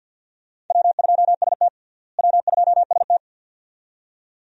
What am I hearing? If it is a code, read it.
Morse code, heard as a keyed tone, about 53 wpm, 700 Hz. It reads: W2ST W2ST
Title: USA Call Signs - 4 Characters - Repeated 2x - 50wpm